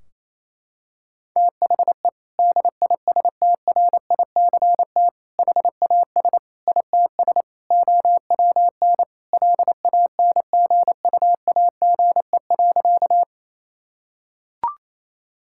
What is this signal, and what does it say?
Morse code, heard as a keyed tone, about 27 wpm, 700 Hz. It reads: THE DISTRICT HAS ITS OWN LANGUAGE.
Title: Sentences from Top 1600 Words 28wpm